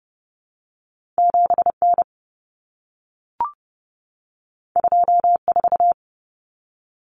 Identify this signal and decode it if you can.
Morse code, heard as a keyed tone, about 30 wpm, 700 Hz. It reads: 7N 24
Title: Common Cut Numbers - 2-Digit 30wpm